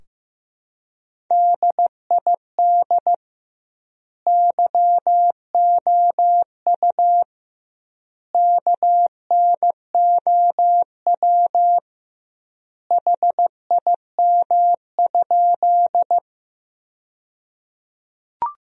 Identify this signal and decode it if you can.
Morse code, heard as a keyed tone, about 15 wpm, 700 Hz. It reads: DID YOU KNOW HIM?
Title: 2x Word-Spacing - Sentences from Top 100 Words - 15wpm